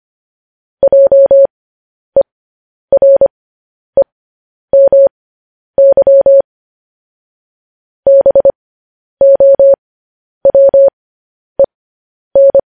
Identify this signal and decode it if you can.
Morse code, heard as a keyed tone, about 25 wpm, spaced as for 10 wpm, 550 Hz. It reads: JEREMY BOWEN
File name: podcast / News headlines-1x-25x10